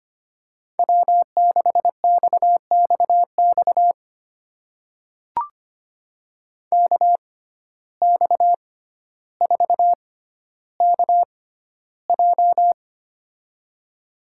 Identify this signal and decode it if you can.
Morse code, heard as a keyed tone, about 25 wpm, spaced as for 13 wpm, 700 Hz. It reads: W6XXX KX4KJ
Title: USA Call Signs - 5 Characters - 6x Character Spacing - 25wpm